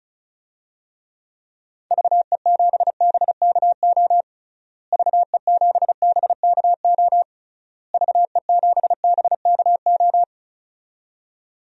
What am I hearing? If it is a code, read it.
Morse code, heard as a keyed tone, about 34 wpm, 700 Hz. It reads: VE7BKO VE7BKO VE7BKO
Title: DXCC-100 Call Signs - 6 Characters - Repeated 3x - 35wpm